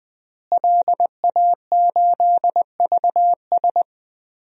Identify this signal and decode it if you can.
Morse code, heard as a keyed tone, about 20 wpm, 700 Hz. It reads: LA8VS
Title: DXCC-100 Call Signs - 5 Characters - Repeated 3x - 20wpm